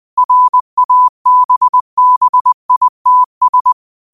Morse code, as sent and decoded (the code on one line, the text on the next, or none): .-. .- -... -... .. - ...
RABBITS